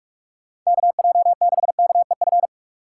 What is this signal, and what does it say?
Morse code, heard as a keyed tone, about 46 wpm, 700 Hz. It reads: KJ6KEF
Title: USA Call Signs - 6 Characters - 2x Character Spacing - 45wpm